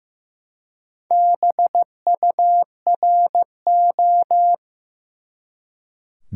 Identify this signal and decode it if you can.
Morse code, heard as a keyed tone, about 15 wpm, 700 Hz. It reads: BURO
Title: Top 180 QSO Elements 15wpm